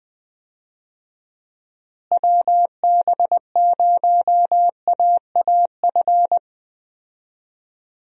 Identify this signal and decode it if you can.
Morse code, heard as a keyed tone, about 20 wpm, 700 Hz. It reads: WB0AAF
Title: USA Call Signs - 6 Characters - Repeated 3x - 20wpm